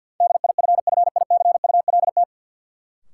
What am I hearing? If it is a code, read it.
Morse code, heard as a keyed tone, about 52 wpm, 700 Hz. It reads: DIFFICULT
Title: Sets of 4 Words - New Word Drill - 50wpm